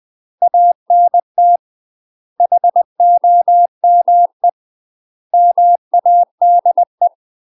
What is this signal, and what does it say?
Morse code, heard as a keyed tone, about 20 wpm, 700 Hz. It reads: ANT HOME MADE
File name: UK MASTER QSOs TRAINING / 20WPM / UK MASTER QSOs 0060